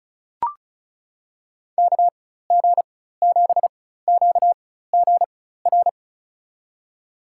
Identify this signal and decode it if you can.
Morse code, heard as a keyed tone, about 36 wpm, spaced as for 16 wpm, 700 Hz. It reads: KG7QGR
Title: USA Call Signs - 4x Character Spacing - 35wpm